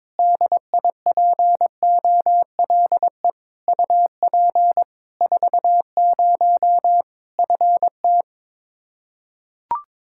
Morse code, heard as a keyed tone, about 21 wpm, 700 Hz. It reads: DIPOLE UP 40 FT
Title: K1USN SST QSO Practice 22wpm